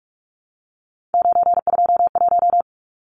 Morse code, heard as a keyed tone, about 44 wpm, 700 Hz. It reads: 921
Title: Numbers - 3-Digit 45wpm